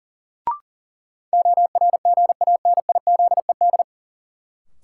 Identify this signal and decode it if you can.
Morse code, heard as a keyed tone, about 40 wpm, 700 Hz. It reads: ORGANIZED
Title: Sentences from Top 1300 Words - New Word Drill Alt - 40wpm